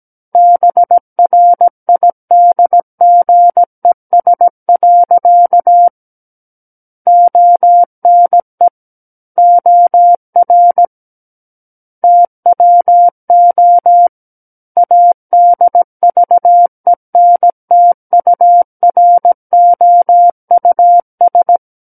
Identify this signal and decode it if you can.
Morse code, heard as a keyed tone, about 17 wpm, 700 Hz. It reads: BRIDGES. ONE OR TWO ADVENTUROUS
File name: Book One / 08-Chapter 8 - 17 WPM